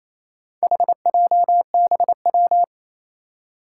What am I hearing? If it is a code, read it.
Morse code, heard as a keyed tone, about 27 wpm, 700 Hz. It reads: HJBW
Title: Four Challenging Letter-Number Combinations 28wpm